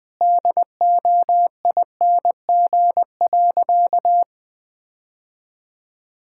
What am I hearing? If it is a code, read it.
Morse code, heard as a keyed tone, about 21 wpm, 700 Hz. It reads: DOING.
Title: Sentences with 4-Word Phrases - 20wpm